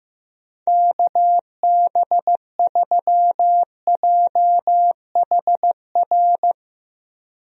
Morse code, heard as a keyed tone, about 15 wpm, 700 Hz. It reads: KB3JHR
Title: USA Call Signs - 6 Characters - Repeated 2x - 15wpm